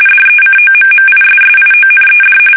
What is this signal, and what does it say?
RTTY, about 45.45 baud. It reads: DECOMMISSIONED
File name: fsk850 / battleshipnj